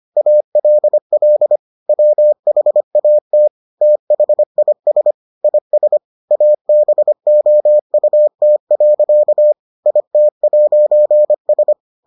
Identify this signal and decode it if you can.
Morse code, heard as a keyed tone, about 25 wpm, 600 Hz. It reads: ALL WHAT THIS IS ABOUT. IT'S